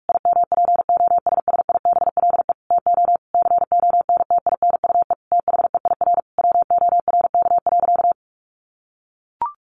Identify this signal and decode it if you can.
Morse code, heard as a keyed tone, about 44 wpm, 700 Hz. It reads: IMPOSSIBLE TO CONTINUE THEIR WORK.
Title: Sentences from Top 1100 Words 45wpm